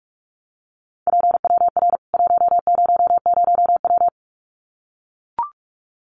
Morse code, heard as a keyed tone, about 44 wpm, 700 Hz. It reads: PWR 100W
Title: Common QSO Phrases 45wpm